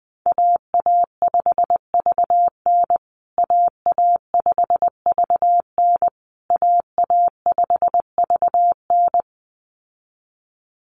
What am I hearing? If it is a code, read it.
Morse code, heard as a keyed tone, about 20 wpm, 700 Hz. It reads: AA5VN AA5VN AA5VN
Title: USA Call Signs Repeated 3x 20wpm